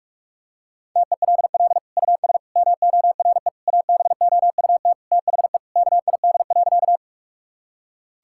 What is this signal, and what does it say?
Morse code, heard as a keyed tone, about 41 wpm, 700 Hz. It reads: TELL US MORE ABOUT THE KID.
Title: Sentences from Top 1600 Words 45wpm